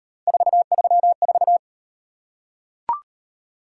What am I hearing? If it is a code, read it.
Morse code, heard as a keyed tone, about 38 wpm, 700 Hz. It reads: 434